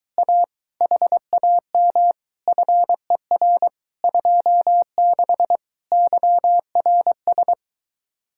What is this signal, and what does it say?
Morse code, heard as a keyed tone, about 22 wpm, 700 Hz. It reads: A HAM FER 26 YRS